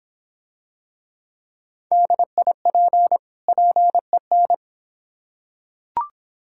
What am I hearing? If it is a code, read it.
Morse code, heard as a keyed tone, about 26 wpm, 700 Hz. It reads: DIP PEN